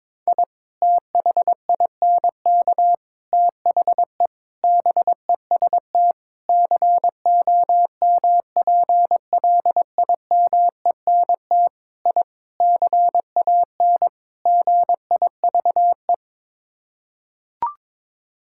Morse code, heard as a keyed tone, about 22 wpm, 700 Hz. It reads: I THINK THE BEST COMPLIMENT I CAN GIVE
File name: practice / 22wpm / Sets of 8 Words - 22wpm